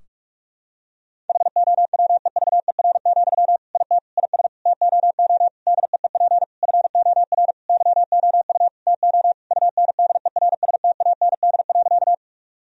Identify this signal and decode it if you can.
Morse code, heard as a keyed tone, about 48 wpm, 700 Hz. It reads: HOWEVER, IT IS TOO DEEP FOR YOU TO UNDERSTAND.
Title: Sentences from Top 1000 Words - Review Part 3 - 45wpm